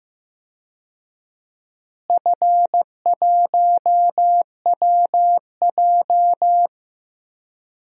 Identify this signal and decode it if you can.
Morse code, heard as a keyed tone, about 15 wpm, 700 Hz. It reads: F1WJ
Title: DXCC-100 Call Signs - 4 Characters - Repeated 3x - 15wpm